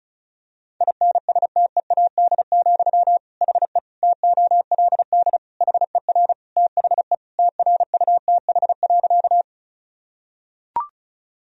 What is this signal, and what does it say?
Morse code, heard as a keyed tone, about 34 wpm, 700 Hz. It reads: INSTEAD, HE TOLD HER THE TRUTH.
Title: Sentences from Top 1000 Words - Review Part 4 - 35wpm